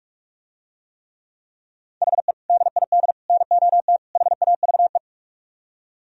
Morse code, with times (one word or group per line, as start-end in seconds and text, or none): 2.0-2.3 HE
2.5-3.1 DID
3.3-4.0 NOT
4.2-5.0 HAVE